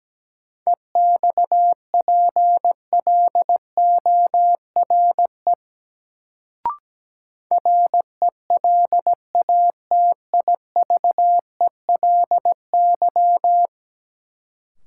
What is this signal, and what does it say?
Morse code, heard as a keyed tone, about 17 wpm, 700 Hz. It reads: EXPLORE RELATIVELY
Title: Sentences from Top 1100 Words - New Word Drill - 17wpm